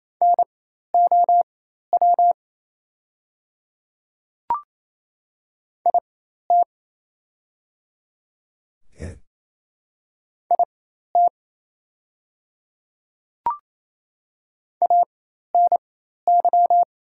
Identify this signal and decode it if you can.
Morse code, heard as a keyed tone, about 28 wpm, spaced as for 13 wpm, 700 Hz. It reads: NOW IT IT ANY
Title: 4x Character Spacing - Top 300 Words - 28wpm